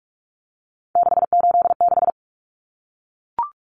Morse code, heard as a keyed tone, about 44 wpm, 700 Hz. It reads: 686
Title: Common Cut Numbers - 3-Digit 45wpm